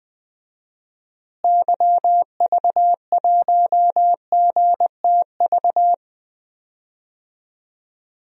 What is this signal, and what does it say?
Morse code, heard as a keyed tone, about 20 wpm, 700 Hz. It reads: YV1GTV
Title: DXCC-100 Call Signs - 6 Characters - Repeated 2x - 20wpm